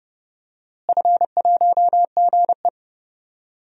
Morse code, heard as a keyed tone, about 31 wpm, 700 Hz. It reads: F1GE
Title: DXCC-100 Call Signs - 4 Characters - Repeated 3x - 30wpm